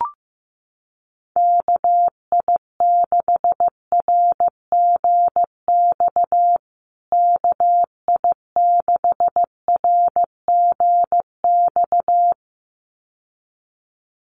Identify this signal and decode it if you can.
Morse code, heard as a keyed tone, about 15 wpm, 700 Hz. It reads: KI6RGX KI6RGX
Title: USA Call Signs Repeated 2x 15wpm